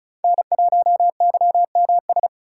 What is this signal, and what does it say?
Morse code, heard as a keyed tone, about 36 wpm, 700 Hz. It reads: N1YMS
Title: USA Call Signs - 5 Characters - 4x Character Spacing - 35wpm